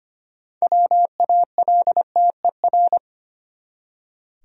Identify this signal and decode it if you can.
Morse code, heard as a keyed tone, about 25 wpm, 700 Hz. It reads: WALTER
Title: USA Common Names - 25wpm